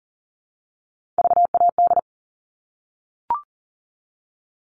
Morse code, heard as a keyed tone, about 40 wpm, 700 Hz. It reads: VAD
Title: All Cut Numbers - 3-Digit 40wpm